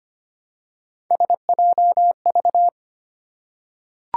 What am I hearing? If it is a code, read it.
Morse code, heard as a keyed tone, about 25 wpm, 700 Hz. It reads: SJV
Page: morse-code-ninja-practice-25wpm